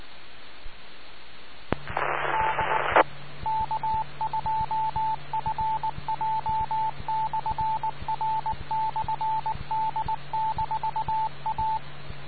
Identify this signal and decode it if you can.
Morse code, heard as a keyed tone, about 19 wpm, 880 Hz. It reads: DE K2FJ/R/D-A